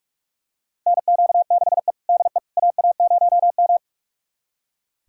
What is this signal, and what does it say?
Morse code, heard as a keyed tone, about 46 wpm, 700 Hz. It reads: NQ6E DE AA0M
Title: USA Callsign DE Callsign - 4 Characters - 45wpm